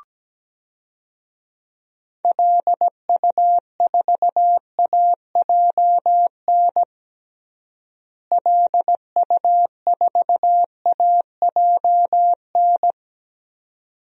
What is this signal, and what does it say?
Morse code, heard as a keyed tone, about 17 wpm, 700 Hz. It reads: LU4AJN LU4AJN
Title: DXCC-100 Call Signs - 6 Characters - Repeated 2x - 17wpm